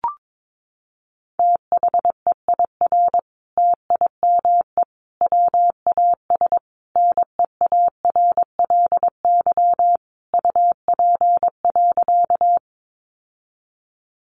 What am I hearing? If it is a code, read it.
Morse code, heard as a keyed tone, about 22 wpm, 700 Hz. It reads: THEIR TIME WAS NEARLY UP.
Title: Sentences from Top 1300 Words 22wpm